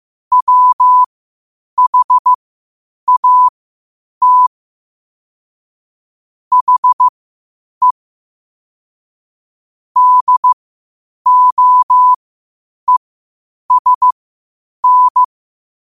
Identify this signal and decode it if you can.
Morse code, heard as a keyed tone, about 15 wpm, spaced as for 8 wpm, 1000 Hz. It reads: WHAT HE DOESN
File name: qotdfiles / 8wpm